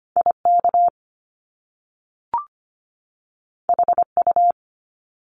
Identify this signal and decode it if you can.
Morse code, heard as a keyed tone, about 25 wpm, 700 Hz. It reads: IK HU
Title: Top Two Letter Combinations in Words 25wpm